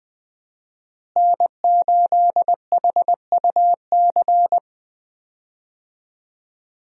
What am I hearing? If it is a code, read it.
Morse code, heard as a keyed tone, about 20 wpm, 700 Hz. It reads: N8HUC
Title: USA Call Signs - 5 Characters - Repeated 3x - 20wpm